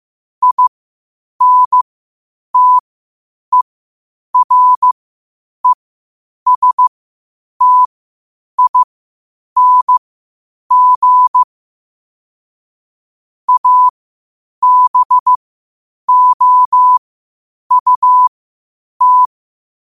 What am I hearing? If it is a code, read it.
Morse code, heard as a keyed tone, about 15 wpm, spaced as for 8 wpm, 1000 Hz. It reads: INTERESTING ABOUT
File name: qotdfiles / 8wpm